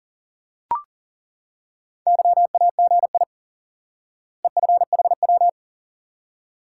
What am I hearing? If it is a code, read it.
Morse code, heard as a keyed tone, about 40 wpm, 700 Hz. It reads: YAGI EFHW
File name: IQE Course 3 - Lesson 89 - YAGI - Two Elements - 40wpm